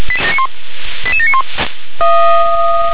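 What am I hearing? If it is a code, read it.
Five-tone selective call (ZVEI), heard as quick runs of five tones, tones of about 70 ms. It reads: 88791
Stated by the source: ZVEI five-tone call received by SDR.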